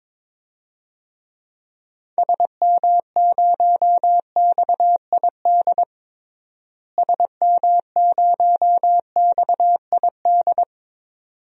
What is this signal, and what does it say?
Morse code, heard as a keyed tone, about 22 wpm, 700 Hz. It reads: SM0XID SM0XID
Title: DXCC-100 Call Signs - 6 Characters - Repeated 2x - 22wpm